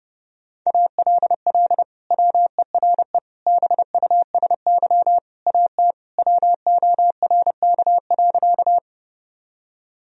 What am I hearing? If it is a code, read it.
Morse code, heard as a keyed tone, about 30 wpm, 700 Hz. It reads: ALL WERE BUSY AT WORK.
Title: Sentences from Top 1000 Words - Review Part 3 - 30wpm